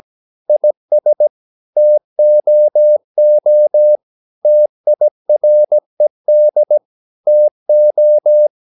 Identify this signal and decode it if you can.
Morse code, heard as a keyed tone, about 17 wpm, 600 Hz. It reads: IS TOO TIRED TO
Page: Romanen Never af Ken Follet på engelsk.